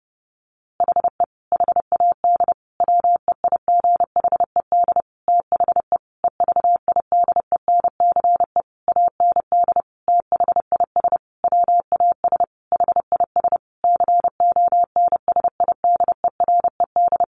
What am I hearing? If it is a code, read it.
Morse code, heard as a keyed tone, about 28 wpm, 700 Hz. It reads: HE HAD WEIGHED THE EVIDENCE AND THIS WAS HIS CONSIDERED
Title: A Sound Of Thunder 30wpm